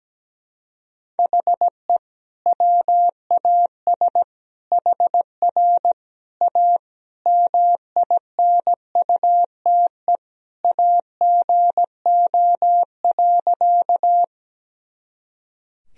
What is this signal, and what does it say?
Morse code, heard as a keyed tone, about 17 wpm, 700 Hz. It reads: HE WAS HR A MINUTE AGO.
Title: Sentences from Top 500 Words and Abbreviations 17wpm